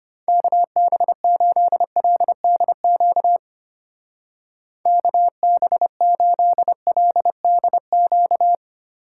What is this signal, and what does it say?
Morse code, heard as a keyed tone, about 32 wpm, 700 Hz. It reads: KB8LDQ KB8LDQ